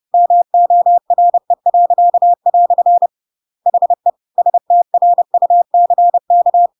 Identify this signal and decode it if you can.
Morse code, heard as a keyed tone, about 30 wpm, 700 Hz. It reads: MORE." HE STRUCK
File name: Pinocchio / Pinocchio ch1 30 30 ik1hge